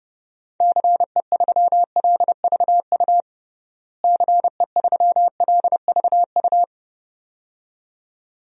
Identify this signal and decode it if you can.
Morse code, heard as a keyed tone, about 30 wpm, 700 Hz. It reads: CE3LVU CE3LVU
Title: DXCC-100 Call Signs - 6 Characters - Repeated 2x - 30wpm